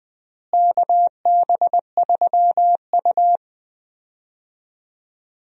Morse code, heard as a keyed tone, about 20 wpm, 700 Hz. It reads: KB3U
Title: USA Call Signs - 4 Characters - Repeated 2x - 20wpm